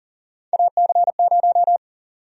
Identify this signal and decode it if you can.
Morse code, heard as a keyed tone, about 40 wpm, 700 Hz. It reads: AC0
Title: USA Call Sign Prefixes - 40wpm